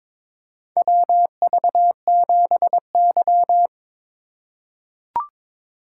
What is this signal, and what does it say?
Morse code, heard as a keyed tone, about 22 wpm, 700 Hz. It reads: WV7Y
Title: USA Call Signs - 4x Character Spacing - 22wpm